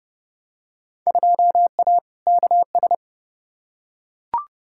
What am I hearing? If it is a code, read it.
Morse code, heard as a keyed tone, about 31 wpm, 700 Hz. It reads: 2A KS
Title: Field Day Exchanges - 30wpm